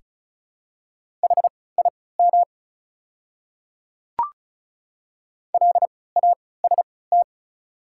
Morse code, heard as a keyed tone, about 36 wpm, spaced as for 20 wpm, 690 Hz. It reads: HIM LAST
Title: 3x Character Spacing - Top 500 Words - 35wpm